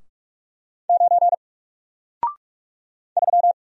Morse code, heard as a keyed tone, about 44 wpm, 700 Hz. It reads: 9 3
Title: Single Letter-Number 45wpm